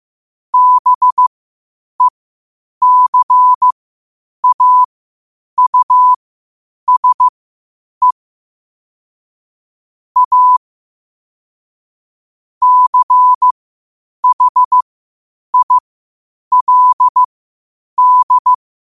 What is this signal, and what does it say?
Morse code, heard as a keyed tone, about 15 wpm, spaced as for 8 wpm, 1000 Hz. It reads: BECAUSE A CHILD
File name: qotdfiles / 8wpm